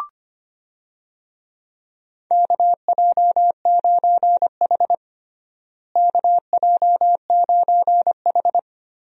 Morse code, heard as a keyed tone, about 25 wpm, 700 Hz. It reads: KJ9H KJ9H
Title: USA Call Signs - 4 Characters - Repeated 2x - 25wpm